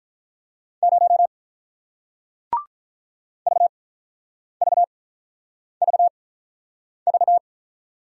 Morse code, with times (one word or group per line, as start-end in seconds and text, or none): 0.8-1.3 0
3.5-7.4 VVVV